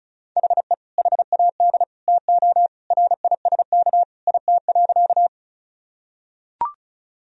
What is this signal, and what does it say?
Morse code, heard as a keyed tone, about 34 wpm, 700 Hz. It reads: HE HAD TO RISK IT.
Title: Sentences from Top 1000 Words - Review Part 4 - 35wpm